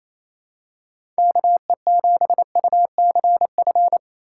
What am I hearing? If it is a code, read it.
Morse code, heard as a keyed tone, about 28 wpm, 700 Hz. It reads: KE7UCF
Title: USA Call Signs - 6 Characters - 4x Character Spacing - 28wpm